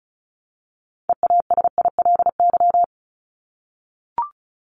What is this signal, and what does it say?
Morse code, heard as a keyed tone, about 34 wpm, 700 Hz. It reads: EASILY
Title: LY Suffix - Top 100 Words with Base Words 35wpm